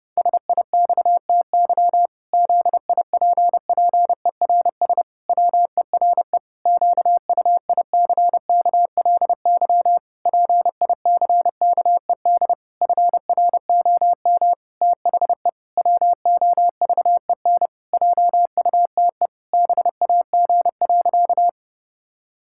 Morse code, heard as a keyed tone, about 32 wpm, 700 Hz. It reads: SIXTY ZIPPERS WERE QUICKLY PICKED FROM THE WOVEN JUTE BAG.
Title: Pangram Challenge v2 - 30wpm